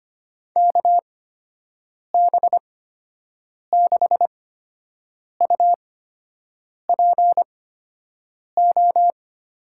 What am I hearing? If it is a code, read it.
Morse code, heard as a keyed tone, about 25 wpm, spaced as for 7 wpm, 700 Hz. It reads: KB6UPO
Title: USA Call Signs - 6 Characters - 8x Character Spacing - 25wpm